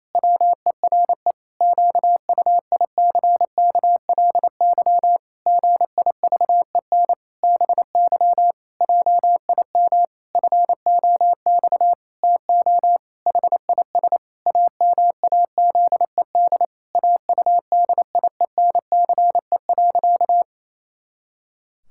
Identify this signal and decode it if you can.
Morse code, heard as a keyed tone, about 28 wpm, 700 Hz. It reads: WERE QUICKLY GIVEN BY JIM FOX TO HIS AMAZED AUDIENCE.
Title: Pangram Challenge v2 - 28wpm